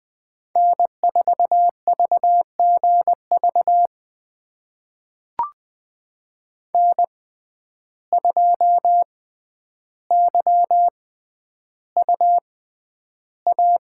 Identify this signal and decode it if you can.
Morse code, heard as a keyed tone, about 20 wpm, spaced as for 10 wpm, 700 Hz. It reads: N4VGV N2YUA
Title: USA Call Signs - 5 Characters - 6x Character Spacing - 20wpm